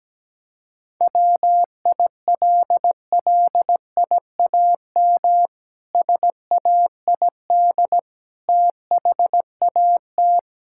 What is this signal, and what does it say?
Morse code, heard as a keyed tone, about 17 wpm, 700 Hz. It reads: WILLIAM SAID THAT
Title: Pangram Challenge v2 - 17wpm